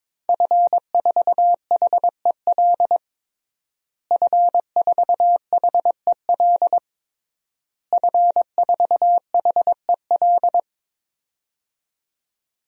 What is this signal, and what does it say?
Morse code, heard as a keyed tone, about 22 wpm, 700 Hz. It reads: F4HEL F4HEL F4HEL
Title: DXCC-100 Call Signs - 5 Characters - Repeated 3x - 22wpm